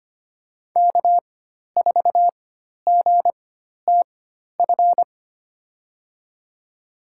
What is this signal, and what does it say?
Morse code, heard as a keyed tone, about 25 wpm, spaced as for 12 wpm, 700 Hz. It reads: K4GTF
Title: USA Call Signs - 5 Characters - 4x Character Spacing - 25wpm